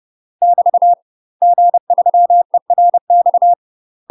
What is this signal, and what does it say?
Morse code, heard as a keyed tone, about 30 wpm, 700 Hz. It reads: <BT> G3ERX
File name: UK MASTER QSOs TRAINING / 30WPM / UK MASTER QSOs 0047